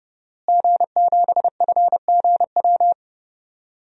Morse code, heard as a keyed tone, about 32 wpm, 700 Hz. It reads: G7FGW
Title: DXCC-100 Call Signs - 5 Characters - Repeated 2x - 30wpm